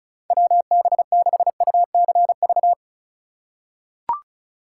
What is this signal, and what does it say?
Morse code, heard as a keyed tone, about 36 wpm, 700 Hz. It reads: WB6UCV
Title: USA Call Signs - 2x Character Spacing - 35wpm